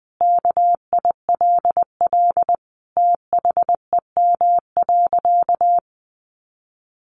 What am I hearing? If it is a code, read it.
Morse code, heard as a keyed tone, about 20 wpm, 700 Hz. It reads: KILL THEM.
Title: Sentences from Top 1400 Words 20wpm